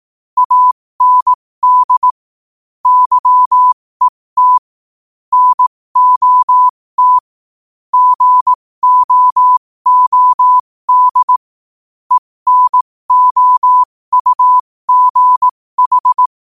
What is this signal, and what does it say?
Morse code, heard as a keyed tone, about 18 wpm, spaced as for 15 wpm, 1000 Hz. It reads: AND YET NOT GOOD ENOUGH
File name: qotdfiles / 15wpm